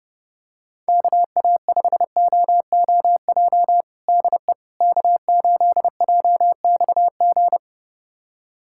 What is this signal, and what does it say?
Morse code, heard as a keyed tone, about 32 wpm, 700 Hz. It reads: KA5OOJ DE K8JXG
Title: USA Callsign DE Callsign 30wpm